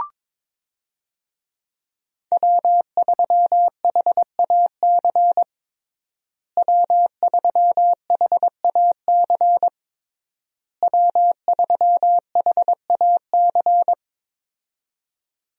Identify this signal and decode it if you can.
Morse code, heard as a keyed tone, about 22 wpm, 690 Hz. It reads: W3HAC W3HAC W3HAC
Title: USA Call Signs - 5 Characters - Repeated 3x - 22wpm